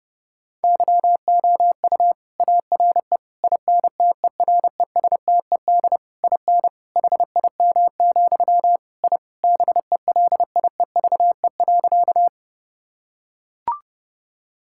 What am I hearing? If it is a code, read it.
Morse code, heard as a keyed tone, about 29 wpm, 700 Hz. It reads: YOU ARE INTERESTED IN HIM, I BELIEVE.
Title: Sentences from Top 1000 Words - Review Part 6 - 30wpm